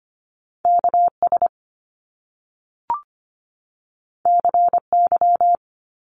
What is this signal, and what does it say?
Morse code, heard as a keyed tone, about 25 wpm, 700 Hz. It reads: KS CY
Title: Top Two Letter Combinations in Words 25wpm